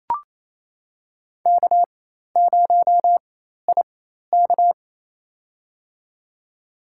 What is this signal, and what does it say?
Morse code, heard as a keyed tone, about 28 wpm, spaced as for 13 wpm, 700 Hz. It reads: K0IK
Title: USA Call Signs - 4x Character Spacing - 28wpm